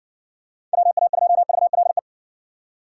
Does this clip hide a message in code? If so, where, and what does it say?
Yes: Morse code from 0.7 to 2.0 s, heard as a keyed tone, about 58 wpm, 700 Hz. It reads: WA2VLE